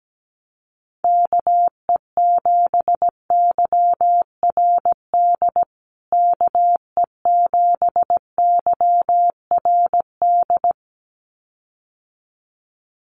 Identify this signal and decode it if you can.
Morse code, heard as a keyed tone, about 17 wpm, 700 Hz. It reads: KE7YRD KE7YRD
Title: USA Call Signs Repeated 2x 17wpm